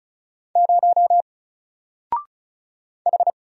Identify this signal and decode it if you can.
Morse code, heard as a keyed tone, about 36 wpm, 700 Hz. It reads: 0 H
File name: Single-Letter-Number-35wpm